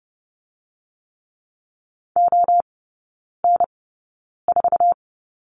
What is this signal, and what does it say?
Morse code, heard as a keyed tone, about 30 wpm, spaced as for 9 wpm, 700 Hz. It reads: ON4
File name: 081 Lesson 14e - Number 2 - Three Characters 30wpm